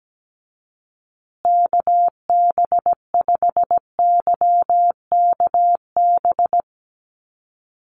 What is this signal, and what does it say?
Morse code, heard as a keyed tone, about 17 wpm, 700 Hz. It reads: KB5YKB
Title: USA Call Signs Repeated 2x 17wpm